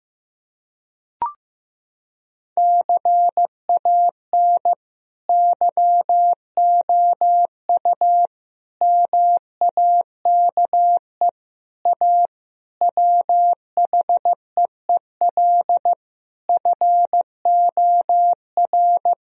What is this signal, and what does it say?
Morse code, heard as a keyed tone, about 16 wpm, 700 Hz. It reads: CAN YOU MAKE A WHEEL FOR
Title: Sentences from Top 1000 Words - Review Part 3 - 15wpm